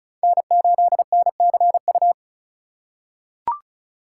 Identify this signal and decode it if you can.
Morse code, heard as a keyed tone, about 36 wpm, 700 Hz. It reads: N8NCU
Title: USA Call Signs - 8x Character Spacing - 35wpm